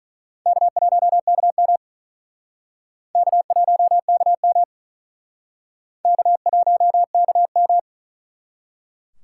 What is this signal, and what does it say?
Morse code, heard as a keyed tone, about 41 wpm, 700 Hz. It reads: K1KM K1KM K1KM
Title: Speed Racing - USA 1x2 Call Signs 35wpm